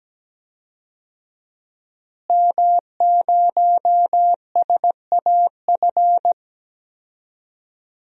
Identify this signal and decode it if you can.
Morse code, heard as a keyed tone, about 17 wpm, 700 Hz. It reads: M0SAF
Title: DXCC-100 Call Signs - 5 Characters - Repeated 2x - 17wpm